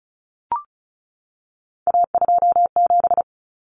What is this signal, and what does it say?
Morse code, heard as a keyed tone, about 34 wpm, 700 Hz. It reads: A27
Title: Common Cut Numbers - 3-Digit 35wpm